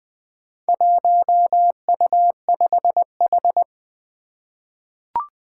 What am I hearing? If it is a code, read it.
Morse code, heard as a keyed tone, about 20 wpm, 700 Hz. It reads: 1U5H
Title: Four Challenging Letter-Number Combinations 20wpm